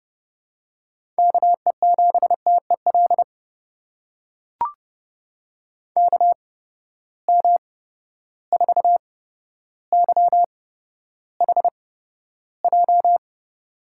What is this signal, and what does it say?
Morse code, heard as a keyed tone, about 30 wpm, spaced as for 13 wpm, 700 Hz. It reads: KE7TEL KM4YHJ
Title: USA Call Signs - 6 Characters - 8x Character Spacing - 30wpm